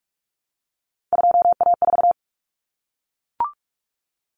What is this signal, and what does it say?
Morse code, heard as a keyed tone, about 44 wpm, 700 Hz. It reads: 2A4
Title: Common Cut Numbers - 3-Digit 45wpm